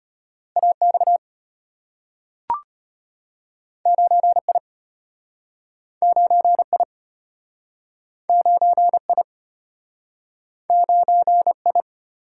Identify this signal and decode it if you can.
Morse code, heard as a keyed tone, about 34 wpm, 700 Hz. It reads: AX 9I 9I 9I 9I